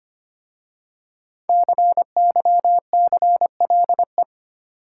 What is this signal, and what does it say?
Morse code, heard as a keyed tone, about 25 wpm, 700 Hz. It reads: CYCLE
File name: 5 Letter Words - First 3 Letters Given - 25wpm